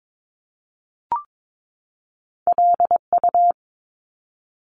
Morse code, heard as a keyed tone, about 22 wpm, 700 Hz. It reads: LU
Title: Top Two Letter Combinations in Words 22wpm